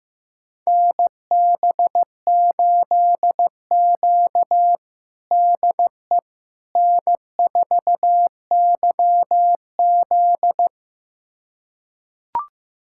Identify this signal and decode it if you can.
Morse code, heard as a keyed tone, about 15 wpm, 700 Hz. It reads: NB8Q DE N4YZ
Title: USA Callsign DE Callsign - 4 Characters - 15wpm